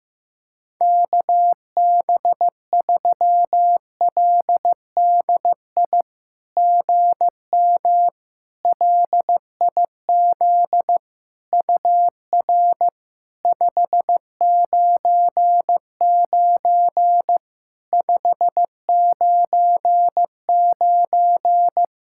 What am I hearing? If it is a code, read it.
Morse code, heard as a keyed tone, about 15 wpm, 700 Hz. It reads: KB3LDI GM LIZ UR 599 599
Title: SOTA QSO Practice 15wpm